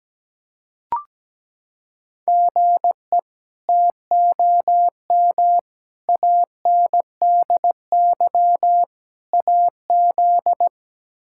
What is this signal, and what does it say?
Morse code, heard as a keyed tone, about 17 wpm, 700 Hz. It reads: GE TOM ANDY AZ
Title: K1USN SST QSO Practice - 17wpm